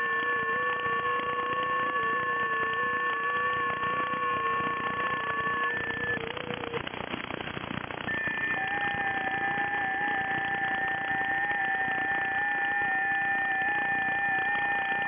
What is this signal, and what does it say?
FT8.
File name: IC-7100